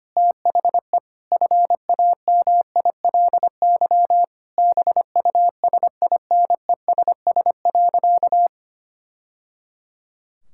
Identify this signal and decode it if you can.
Morse code, heard as a keyed tone, about 25 wpm, 700 Hz. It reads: THE FAMILY BUSINESS.
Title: Sentences from Top 1000 Words - Review Part 6 - 25wpm